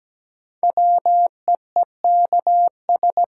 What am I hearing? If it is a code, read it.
Morse code, heard as a keyed tone, about 17 wpm, 700 Hz. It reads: WEEKS
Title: Sets of 5 Words - New Word Drill - 17wpm